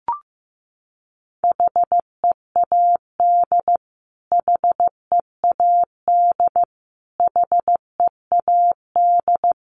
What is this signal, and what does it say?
Morse code, heard as a keyed tone, about 15 wpm, 700 Hz. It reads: HEAD HEAD HEAD
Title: Top 300 Words Repeated 15wpm